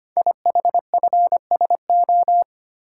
Morse code, heard as a keyed tone, about 25 wpm, 700 Hz. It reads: IHFSO
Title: Code Groups - 5 Letters - 25wpm